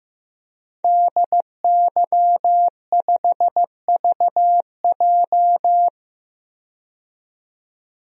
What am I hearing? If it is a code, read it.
Morse code, heard as a keyed tone, about 15 wpm, 700 Hz. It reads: DY5VJ
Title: DXCC-100 Call Signs - 5 Characters - 15wpm